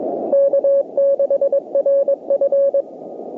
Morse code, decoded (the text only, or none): K6RF